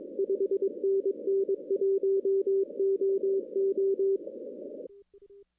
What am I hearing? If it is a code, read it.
Morse code, heard as a keyed tone, about 20 wpm, 380 Hz. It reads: 5NN 1OO
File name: 2016 arrl dx cw / 2016-ARRL-DX-CW-K5TR- - 0-26